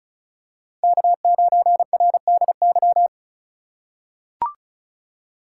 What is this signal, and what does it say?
Morse code, heard as a keyed tone, about 34 wpm, 700 Hz. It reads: K9RDY
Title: USA Call Signs - 5 Characters - 4x Character Spacing - 35wpm